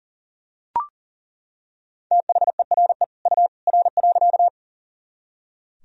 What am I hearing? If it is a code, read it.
Morse code, heard as a keyed tone, about 40 wpm, 700 Hz. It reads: THERE U R.
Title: Sentences from Top 100 Words and Abbreviations 40wpm